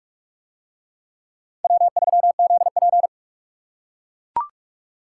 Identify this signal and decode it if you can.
Morse code, heard as a keyed tone, about 44 wpm, 700 Hz. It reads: W3ZP